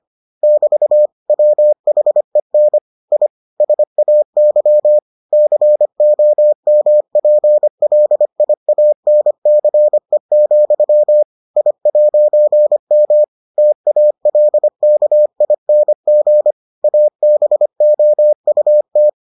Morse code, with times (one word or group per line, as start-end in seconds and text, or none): 0.4-1.1 <BT>
1.3-2.8 WHEN
3.1-3.3 I
3.6-5.0 SAY
5.3-11.2 COMPLIANCE,
11.6-13.2 I'M
13.6-16.5 TALKING
16.8-19.1 ABOUT